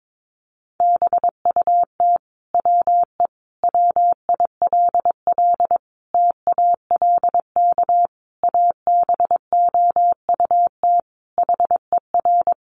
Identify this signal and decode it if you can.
Morse code, heard as a keyed tone, about 22 wpm, 700 Hz. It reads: BUT WE WILL TALK ABOUT HER
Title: Sentences from Top 1200 Words 22wpm